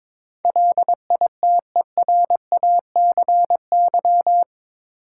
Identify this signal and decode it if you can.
Morse code, heard as a keyed tone, about 22 wpm, 700 Hz. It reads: LITERACY
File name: practice / 22wpm / Eight Letter Words - 22wpm